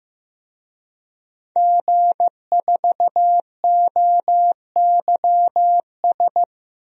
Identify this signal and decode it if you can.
Morse code, heard as a keyed tone, about 15 wpm, 700 Hz. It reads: G4OYS
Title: DXCC-100 Call Signs - 5 Characters - Repeated 2x - 15wpm